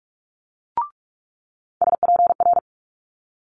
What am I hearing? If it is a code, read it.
Morse code, heard as a keyed tone, about 46 wpm, 700 Hz. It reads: SPR
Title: Multi-letter Phonemes Quick Fire 45wpm